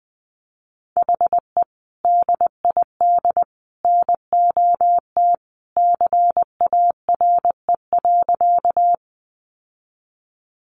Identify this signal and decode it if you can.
Morse code, heard as a keyed tone, about 19 wpm, 700 Hz. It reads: HE DID NOT CARE.
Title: Sentences from Top 600 Words 20wpm